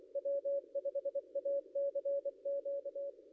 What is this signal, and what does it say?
Morse code, heard as a keyed tone, about 24 wpm, 560 Hz. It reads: W5ACQ
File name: contests / ss-cw-k5tr-op-n5rz -0800